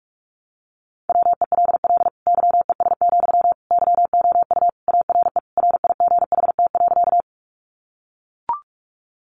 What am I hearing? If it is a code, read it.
Morse code, heard as a keyed tone, about 44 wpm, 700 Hz. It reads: WELL YES, YOU ARE RIGHT.
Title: Sentences from Top 500 Words 45wpm